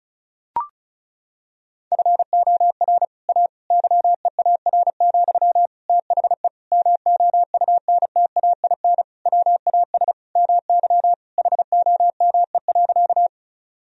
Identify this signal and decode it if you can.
Morse code, heard as a keyed tone, about 34 wpm, 700 Hz. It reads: FOR A YEAR, THE MOUNTAIN WAS MY HOME.
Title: Sentences from Top 1000 Words - Review Part 1 - 35wpm